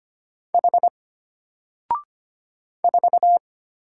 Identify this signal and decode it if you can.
Morse code, heard as a keyed tone, about 25 wpm, 700 Hz. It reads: H 4
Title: Single Letter-Number Alt 25wpm